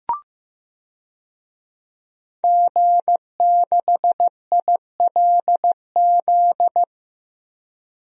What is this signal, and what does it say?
Morse code, heard as a keyed tone, about 15 wpm, 700 Hz. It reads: G6ILZ
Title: DXCC-100 Call Signs - 5 Characters - 15wpm